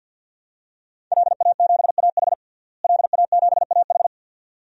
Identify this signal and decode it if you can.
Morse code, heard as a keyed tone, about 50 wpm, 700 Hz. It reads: LA7AH LA7AH
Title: DXCC-100 Call Signs - 5 Characters - Repeated 2x - 50wpm